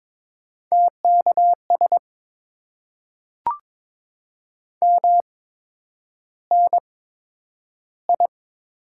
Morse code, heard as a keyed tone, about 22 wpm, spaced as for 9 wpm, 700 Hz. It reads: TKS MNI
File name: Top 90 QSO Elements - 8x Character Spacing - 22wpm